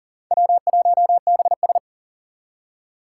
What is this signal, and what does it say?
Morse code, heard as a keyed tone, about 40 wpm, 700 Hz. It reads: W1BS
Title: Four Challenging Letter-Number Combinations 40wpm